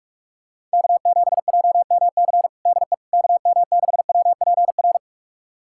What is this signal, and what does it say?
Morse code, heard as a keyed tone, about 42 wpm, 700 Hz. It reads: K7JMC DE KM6WPR
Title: USA Callsign DE Callsign 45wpm